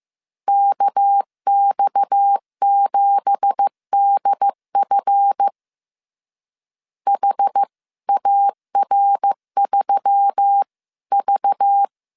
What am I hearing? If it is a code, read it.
Morse code, heard as a keyed tone, about 15 wpm, 800 Hz. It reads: KX7DF HAR3V
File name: CW / DET / 1 9 1500C S C 14wpm 3245